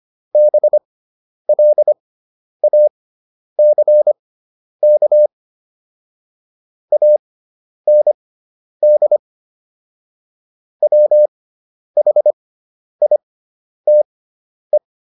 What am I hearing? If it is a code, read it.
Morse code, heard as a keyed tone, about 25 wpm, spaced as for 10 wpm, 600 Hz. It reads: BLACK AND WHITE